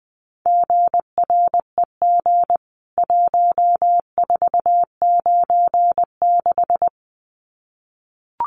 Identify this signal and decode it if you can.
Morse code, heard as a keyed tone, about 20 wpm, 700 Hz. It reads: GREG 1496
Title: CWT Exchange 20wpm